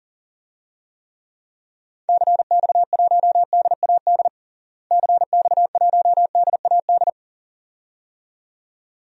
Morse code, heard as a keyed tone, about 37 wpm, 700 Hz. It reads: CX1DAD CX1DAD
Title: DXCC-100 Call Signs - 6 Characters - Repeated 2x - 40wpm